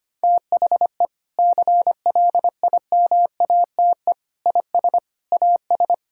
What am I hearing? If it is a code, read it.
Morse code, heard as a keyed tone, about 25 wpm, 700 Hz. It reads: THE CLIMATE IS AS
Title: Sentences from Top 1600 Words 25wpm